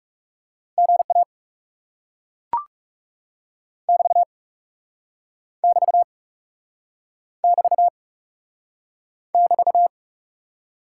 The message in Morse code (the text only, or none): GA <BT> <BT> <BT> <BT>